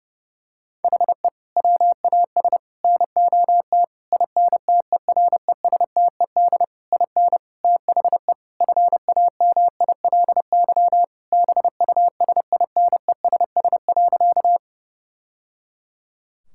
Morse code, heard as a keyed tone, about 32 wpm, 700 Hz. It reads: HE WAS NOT INTERESTED IN THE FAMILY BUSINESS.
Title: Sentences from Top 1000 Words - Review Part 6 - 30wpm